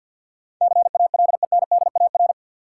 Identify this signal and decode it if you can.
Morse code, heard as a keyed tone, about 53 wpm, 700 Hz. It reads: CALENDAR
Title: Sentences from Top 1600 Words - New Word Drill Alt - 50wpm